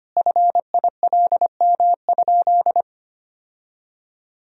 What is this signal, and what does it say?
Morse code, heard as a keyed tone, about 25 wpm, 700 Hz. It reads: FILM?
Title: Sentences from Top 1000 Words - Review Part 3 - 25wpm